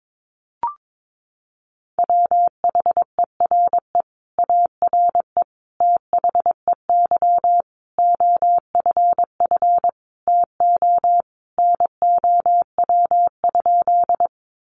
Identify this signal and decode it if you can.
Morse code, heard as a keyed tone, about 22 wpm, 700 Hz. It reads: WHERE ARE THEY OFF TO NOW?
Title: Sentences from Top 200 Words 22wpm